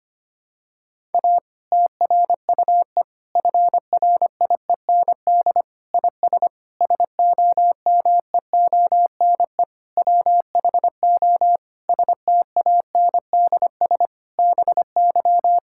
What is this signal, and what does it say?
Morse code, heard as a keyed tone, about 25 wpm, 700 Hz. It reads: A TRUE FRIEND IS SOMEONE WHO STANDS BY
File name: practice / 25wpm / Difficult Sentences with Hints - 25wpm